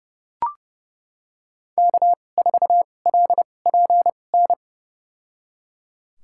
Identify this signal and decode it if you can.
Morse code, heard as a keyed tone, about 30 wpm, spaced as for 22 wpm, 700 Hz. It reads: K4LPN
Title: USA Call Signs - 5 Characters - 2x Character Spacing - 30wpm